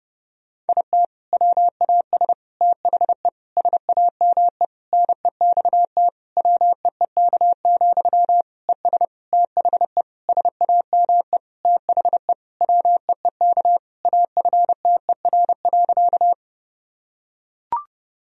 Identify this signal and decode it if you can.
Morse code, heard as a keyed tone, about 32 wpm, 700 Hz. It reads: IT WAS THE SAME NEXT WEEK, ES THE SAME THE WEEK AFTER.
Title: Sentences from Top 500 Words and Abbreviations 30wpm